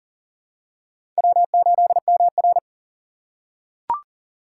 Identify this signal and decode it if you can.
Morse code, heard as a keyed tone, about 39 wpm, 700 Hz. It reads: W8MR
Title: USA Call Signs - 4 Characters - 2x Character Spacing - 40wpm